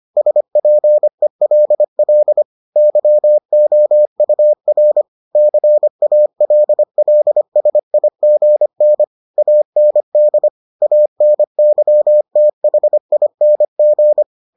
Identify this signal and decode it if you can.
Morse code, heard as a keyed tone, about 25 wpm, 600 Hz. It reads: SPELL YOUR CALLSIGN AND ANYTHING